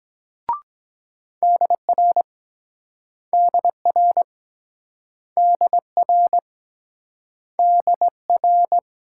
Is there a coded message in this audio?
Yes: Morse code from 1.4 to 8.8 s, heard as a keyed tone, about 23 wpm, 700 Hz. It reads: DR DR DR DR